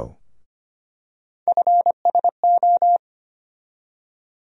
Morse code, heard as a keyed tone, about 25 wpm, 700 Hz. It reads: FSO
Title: USA Call Sign Postfixes - 3 Letters 25wpm